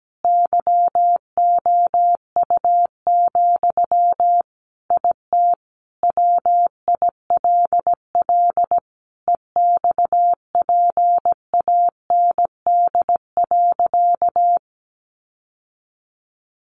Morse code, heard as a keyed tone, about 17 wpm, 700 Hz. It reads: YOU, IT WILL EXPAND.
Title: Sentences from Top 1400 Words 17wpm